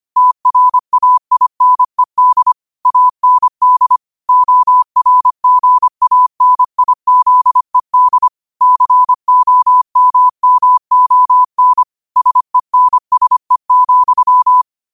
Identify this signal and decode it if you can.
Morse code, heard as a keyed tone, about 25 wpm, 1000 Hz. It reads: TRAINED AND ORGANIZED COMMON SENSE,